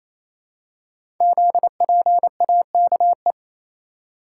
Morse code, heard as a keyed tone, about 28 wpm, 700 Hz. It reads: ZPAKE
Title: Code Groups - 5 Letter:Numbers - 28wpm